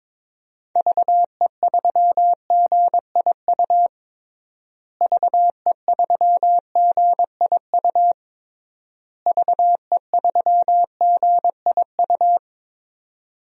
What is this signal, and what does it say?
Morse code, heard as a keyed tone, about 22 wpm, 700 Hz. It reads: VE3GIU VE3GIU VE3GIU
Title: DXCC-100 Call Signs - 6 Characters - Repeated 3x - 22wpm